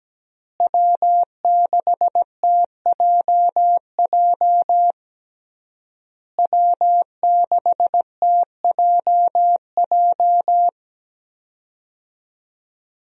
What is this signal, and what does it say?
Morse code, heard as a keyed tone, about 17 wpm, 700 Hz. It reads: W6TJJ W6TJJ
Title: USA Call Signs - 5 Characters - Repeated 3x - 17wpm